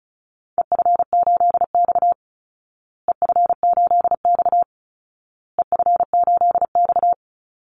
Morse code, heard as a keyed tone, about 34 wpm, 700 Hz. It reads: EF8X EF8X EF8X
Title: Call Signs Repeated 2x - 35wpm